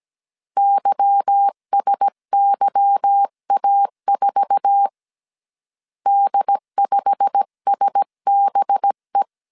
Morse code, heard as a keyed tone, about 17 wpm, 800 Hz. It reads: YSYA4 D5SBE
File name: CW / DET / 1 9 1500C S B 16wpm 2856